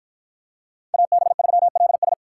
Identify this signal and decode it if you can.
Morse code, heard as a keyed tone, about 50 wpm, 700 Hz. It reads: AB3LS